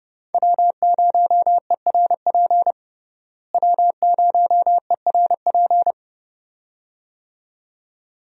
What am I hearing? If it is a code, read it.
Morse code, heard as a keyed tone, about 32 wpm, 700 Hz. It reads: W0ERP W0ERP
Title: USA Call Signs - 5 Characters - Repeated 2x - 30wpm